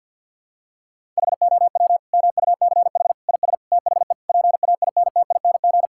Morse code, heard as a keyed tone, about 52 wpm, 700 Hz. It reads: HOW MUCH IS THE PAINTING
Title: Sentences from Top 1000 Words - Review Part 6 - 50wpm